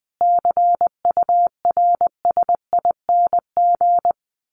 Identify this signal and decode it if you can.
Morse code, heard as a keyed tone, about 20 wpm, 700 Hz. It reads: CURSING
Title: A Sound Of Thunder 20wpm